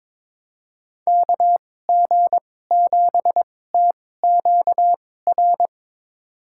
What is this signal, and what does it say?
Morse code, heard as a keyed tone, about 22 wpm, spaced as for 16 wpm, 700 Hz. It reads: KG7TQR
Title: USA Call Signs - 2x Character Spacing - 22wpm